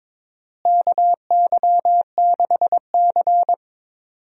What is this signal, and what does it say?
Morse code, heard as a keyed tone, about 22 wpm, 700 Hz. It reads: KY6C
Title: USA Call Signs - 4 Characters - Repeated 2x - 22wpm